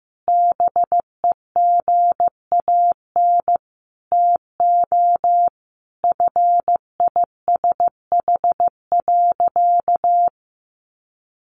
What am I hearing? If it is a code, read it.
Morse code, heard as a keyed tone, about 15 wpm, 700 Hz. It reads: BEGAN TO FISH.
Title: Sentences from Top 300 Words 15wpm